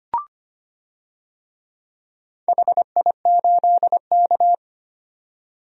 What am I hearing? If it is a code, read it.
Morse code, heard as a keyed tone, about 25 wpm, 700 Hz. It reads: HI8K
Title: DXCC-100 Call Signs - 4 Characters - 25wpm